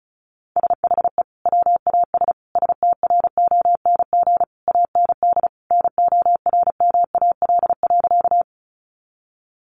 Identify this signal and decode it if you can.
Morse code, heard as a keyed tone, about 34 wpm, 700 Hz. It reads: SHE WAS STRONG AND NORMAL.
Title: Sentences from Top 1200 Words 35wpm